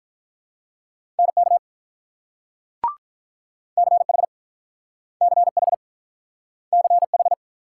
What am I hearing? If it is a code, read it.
Morse code, heard as a keyed tone, about 48 wpm, 700 Hz. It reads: NK CH CH CH